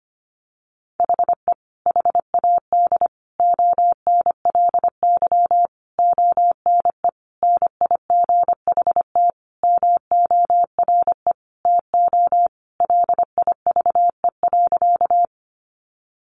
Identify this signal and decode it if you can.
Morse code, heard as a keyed tone, about 25 wpm, 700 Hz. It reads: HE HAD ONLY ONE NIGHT MORE TO LIVE.
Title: Sentences from Top 300 Words 25wpm